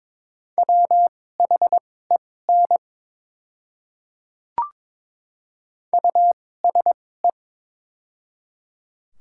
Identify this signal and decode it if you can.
Morse code, heard as a keyed tone, about 22 wpm, spaced as for 16 wpm, 690 Hz. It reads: WHEN USE
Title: 2x Character Spacing - Top 200 Words - 22wpm